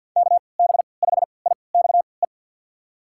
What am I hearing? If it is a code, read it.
Morse code, heard as a keyed tone, about 50 wpm, spaced as for 28 wpm, 700 Hz. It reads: KB5IXE
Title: USA Call Signs - 6 Characters - 3x Character Spacing - 50wpm